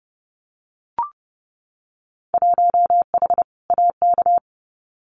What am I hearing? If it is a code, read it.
Morse code, heard as a keyed tone, about 30 wpm, 700 Hz. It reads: 1H AK
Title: Winter Field Day Exchanges Encore 30wpm